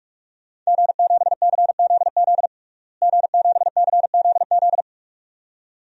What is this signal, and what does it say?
Morse code, heard as a keyed tone, about 45 wpm, 700 Hz. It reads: G7CZZ G7CZZ
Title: DXCC-100 Call Signs - 5 Characters - Repeated 2x - 45wpm